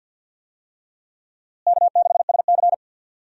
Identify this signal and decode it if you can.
Morse code, heard as a keyed tone, about 52 wpm, 700 Hz. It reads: K6SC
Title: USA Call Signs - 4 Characters - Repeated 2x - 50wpm